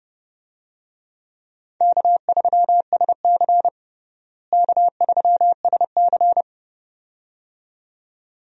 Morse code, heard as a keyed tone, about 31 wpm, 700 Hz. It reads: K3SC K3SC
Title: USA Call Signs - 4 Characters - Repeated 2x - 30wpm